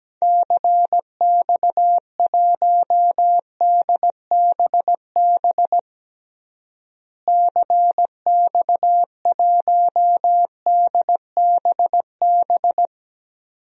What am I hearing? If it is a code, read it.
Morse code, heard as a keyed tone, about 17 wpm, 700 Hz. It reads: CX1DBB CX1DBB
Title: DXCC-100 Call Signs - 6 Characters - Repeated 2x - 17wpm